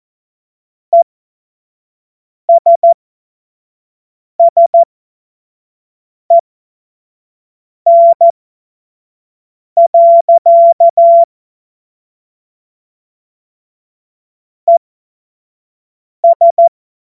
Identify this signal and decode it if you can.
Morse code, heard as a keyed tone, about 14 wpm, spaced as for 5 wpm, 680 Hz. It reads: ESSEN. ES
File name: Morsekurs / Witze / Witz 03. August 2021, 25 F70